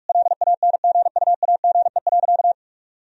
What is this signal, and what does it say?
Morse code, heard as a keyed tone, about 41 wpm, 700 Hz. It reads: LANGUAGE.
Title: Sentences from Top 1000 Words - Review Part 3 - 45wpm